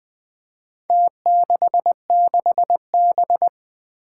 Morse code, heard as a keyed tone, about 20 wpm, 700 Hz. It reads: T66B
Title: Four Challenging Letter-Number Combinations 20wpm